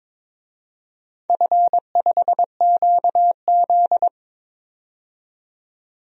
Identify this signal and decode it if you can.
Morse code, heard as a keyed tone, about 22 wpm, 700 Hz. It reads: F5QZ
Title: DXCC-100 Call Signs - 4 Characters - 22wpm